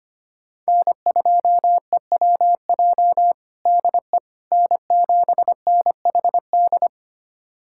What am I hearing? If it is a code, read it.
Morse code, heard as a keyed tone, about 25 wpm, 700 Hz. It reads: N2EWJ DE N7NHD
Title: USA Callsign DE Callsign - 5 Characters - 25wpm